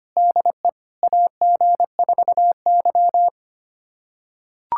Morse code, heard as a keyed tone, about 25 wpm, 700 Hz. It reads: DE AG4Y
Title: USA Callsign DE Callsign - 4 Characters - 25wpm